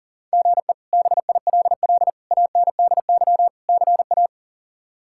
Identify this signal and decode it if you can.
Morse code, heard as a keyed tone, about 37 wpm, 700 Hz. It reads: GE BILL ANDY CA
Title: K1USN SST QSO Practice - 40wpm